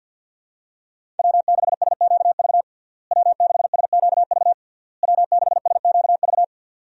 Morse code, heard as a keyed tone, about 52 wpm, 700 Hz. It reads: W6SQV W6SQV W6SQV
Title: USA Call Signs - 5 Characters - Repeated 3x - 50wpm